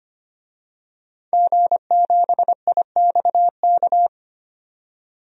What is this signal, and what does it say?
Morse code, heard as a keyed tone, about 25 wpm, 700 Hz. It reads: G7IXK
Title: DXCC-100 Call Signs - 5 Characters - 25wpm